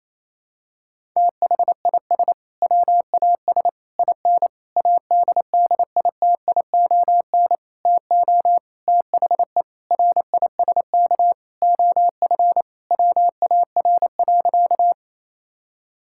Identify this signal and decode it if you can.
Morse code, heard as a keyed tone, about 28 wpm, 700 Hz. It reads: THIS WAS IN ADDITION TO THE RISK OF WAR.
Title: Sentences from Top 1000 Words - Review Part 6 - 28wpm